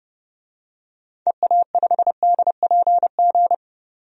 Morse code, heard as a keyed tone, about 32 wpm, 700 Hz. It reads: EA5DPG
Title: DXCC-100 Call Signs - 6 Characters - 30wpm